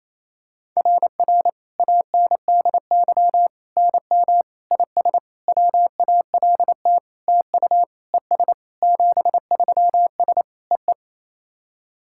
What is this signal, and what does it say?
Morse code, heard as a keyed tone, about 28 wpm, 690 Hz. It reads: RR ANDY NM IS WALT TU ES 73S EE
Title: SOTA QSO Practice 28wpm